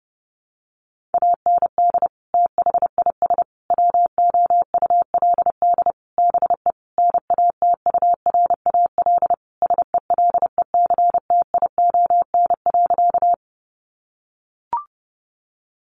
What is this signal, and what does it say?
Morse code, heard as a keyed tone, about 30 wpm, 700 Hz. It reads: AND THIS WOULD BE NATURAL SELECTION.
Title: Sentences from Top 1400 Words 30wpm